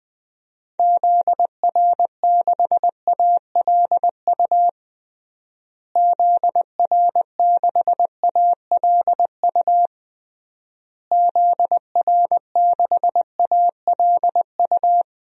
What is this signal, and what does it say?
Morse code, heard as a keyed tone, about 21 wpm, 700 Hz. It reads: ZR6ALU ZR6ALU ZR6ALU
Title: DXCC-100 Call Signs - 6 Characters - Repeated 3x - 20wpm